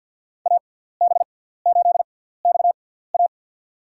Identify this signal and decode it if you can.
Morse code, heard as a keyed tone, about 52 wpm, spaced as for 17 wpm, 690 Hz. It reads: AB8XA
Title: USA Call Signs - 6x Character Spacing - 50wpm